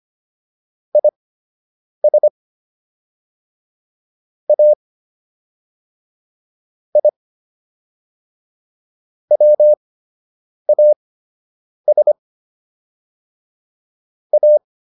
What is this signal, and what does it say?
Morse code, heard as a keyed tone, about 25 wpm, spaced as for 8 wpm, 600 Hz. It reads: IS A I WAS A